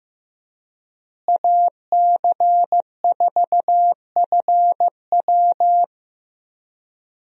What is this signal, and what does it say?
Morse code, heard as a keyed tone, about 15 wpm, 700 Hz. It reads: AC4FW
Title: USA Call Signs - 5 Characters - Repeated 3x - 15wpm